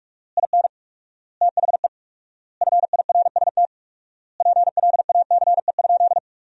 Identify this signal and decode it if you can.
Morse code, heard as a keyed tone, about 47 wpm, 700 Hz. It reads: IN THE FIRST PLACE?
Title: 4x Word-Spacing - Sentences from Top 200 Words - 45wpm